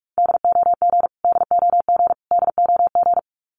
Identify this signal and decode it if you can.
Morse code, heard as a keyed tone, about 44 wpm, 700 Hz. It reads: DOG DOG DOG
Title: Top 500 Words Repeated 45wpm